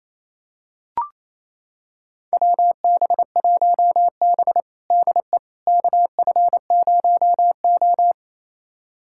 Morse code, heard as a keyed tone, about 28 wpm, 700 Hz. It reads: WB1B DE KF0O
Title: USA Callsign DE Callsign - 4 Characters - 28wpm